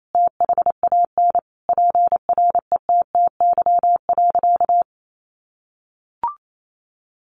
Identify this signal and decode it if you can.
Morse code, heard as a keyed tone, about 28 wpm, 700 Hz. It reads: THAN PRETTY.
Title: Sentences from Top 1000 Words 28wpm